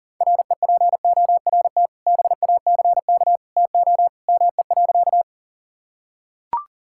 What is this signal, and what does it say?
Morse code, heard as a keyed tone, about 40 wpm, 690 Hz. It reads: REPORT BACK TO ME.
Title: Sentences from Top 1000 Words - Review Part 6 - 40wpm